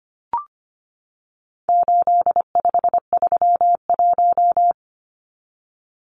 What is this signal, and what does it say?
Morse code, heard as a keyed tone, about 25 wpm, 700 Hz. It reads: 8531
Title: Numbers - 4-Digit 25wpm